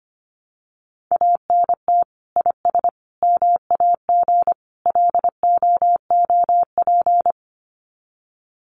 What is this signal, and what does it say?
Morse code, heard as a keyed tone, about 25 wpm, 700 Hz. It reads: ANT IS MAG LOOP
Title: Common QSO Phrases 25wpm